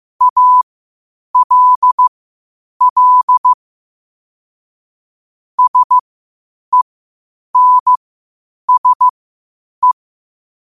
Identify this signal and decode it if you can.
Morse code, heard as a keyed tone, about 15 wpm, spaced as for 8 wpm, 1000 Hz. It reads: ALL SENSE
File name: qotdfiles / 8wpm